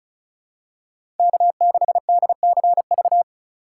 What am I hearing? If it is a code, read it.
Morse code, heard as a keyed tone, about 36 wpm, 700 Hz. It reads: K6DCV
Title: USA Call Signs - 4x Character Spacing - 35wpm